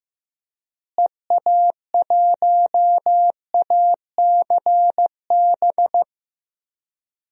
Morse code, heard as a keyed tone, about 15 wpm, 700 Hz. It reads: EA1ACB
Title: DXCC-100 Call Signs - 6 Characters - Repeated 3x - 15wpm